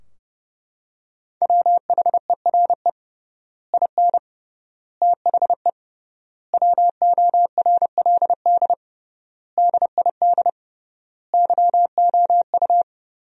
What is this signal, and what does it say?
Morse code, heard as a keyed tone, about 29 wpm, 700 Hz. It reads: WHERE IN THE WORLD DID YOU
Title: 3x Word-Spacing - Sentences from Top 200 Words - 30wpm